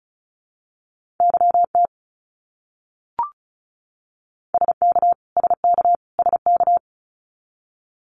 Morse code, heard as a keyed tone, about 34 wpm, 700 Hz. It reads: YT SK SK SK
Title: Canadian Provinces and Territories - Repeated 3x - 35wpm